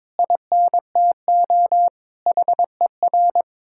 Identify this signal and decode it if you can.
Morse code, heard as a keyed tone, about 22 wpm, 700 Hz. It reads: INTO HER
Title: Sentences from Top 1000 Words - Review Part 6 - 22wpm